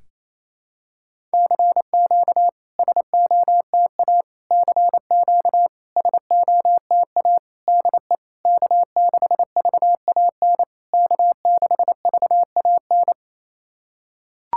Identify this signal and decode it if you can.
Morse code, heard as a keyed tone, about 27 wpm, 660 Hz. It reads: CQ SOTA CQ SOTA DE K6VAN K6VAN
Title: SOTA QSO Practice 28wpm